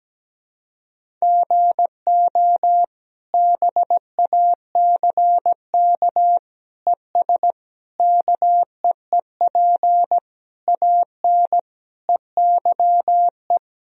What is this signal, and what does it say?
Morse code, heard as a keyed tone, about 17 wpm, 700 Hz. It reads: GO BACK ES KEEP AN EYE
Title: Sentences from Top 300 Words and Abbreviations 17wpm